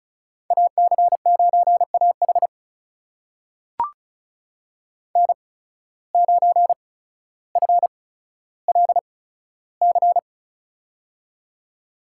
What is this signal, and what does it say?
Morse code, heard as a keyed tone, about 34 wpm, spaced as for 15 wpm, 700 Hz. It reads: AC9AH N9FLC
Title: USA Call Signs - 5 Characters - 8x Character Spacing - 35wpm